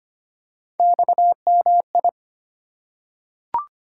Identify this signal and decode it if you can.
Morse code, heard as a keyed tone, about 25 wpm, 700 Hz. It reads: XMI
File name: Three Letter-Number Combinations - 25wpm